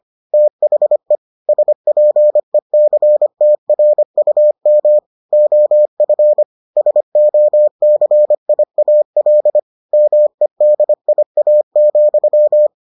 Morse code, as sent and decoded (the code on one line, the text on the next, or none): - .... / ... .--. -.-. - .-. ..- -- / --- ..-. / ... --- -.-. .. .- .-.. / -- . -.. .. .- --..--
THE SPECTRUM OF SOCIAL MEDIA,